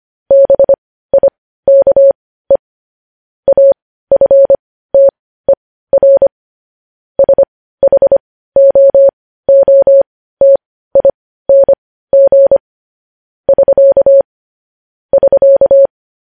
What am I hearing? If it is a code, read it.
Morse code, heard as a keyed tone, about 25 wpm, spaced as for 15 wpm, 550 Hz. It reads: BIKE AFTER SHOOTING <SK> <SK>